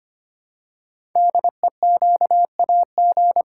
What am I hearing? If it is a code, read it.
Morse code, heard as a keyed tone, about 25 wpm, 700 Hz. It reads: DEQAG
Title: Code Groups - 5 Letters - 25wpm